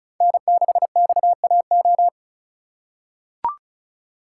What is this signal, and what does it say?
Morse code, heard as a keyed tone, about 34 wpm, 700 Hz. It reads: N6XAO
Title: USA Call Signs - 3x Character Spacing - 35wpm